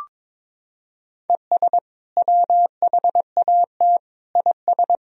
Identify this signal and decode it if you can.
Morse code, heard as a keyed tone, about 22 wpm, 700 Hz. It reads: ES WHAT IS
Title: Sentences from Top 100 Words and Abbreviations 22wpm